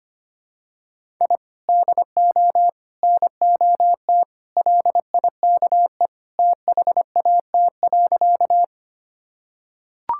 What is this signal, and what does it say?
Morse code, heard as a keyed tone, about 25 wpm, 700 Hz. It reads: I DO NOT LIKE THAT.
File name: practice / 25wpm / Sentences with 3-Word Phrases - 25wpm